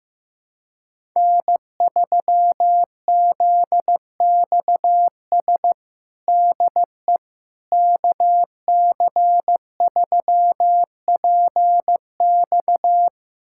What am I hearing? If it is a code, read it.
Morse code, heard as a keyed tone, about 15 wpm, 700 Hz. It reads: N3ZXS DE KC3PX
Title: USA Callsign DE Callsign - 5 Characters - 15wpm